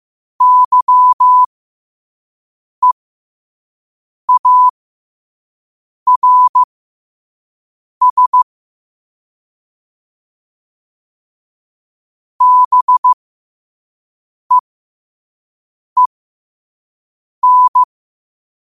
Morse code, heard as a keyed tone, about 15 wpm, spaced as for 5 wpm, 1000 Hz. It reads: YEARS BEEN
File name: qotdfiles / 5wpm